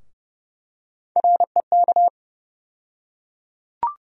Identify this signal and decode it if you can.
Morse code, heard as a keyed tone, about 30 wpm, 700 Hz. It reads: REK
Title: USA Call Sign Postfixes - 3 Letters 30wpm